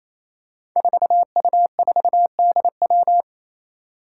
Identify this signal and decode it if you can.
Morse code, heard as a keyed tone, about 28 wpm, 700 Hz. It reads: 4U4DW
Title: Five Challenging Letter-Number Combinations 28wpm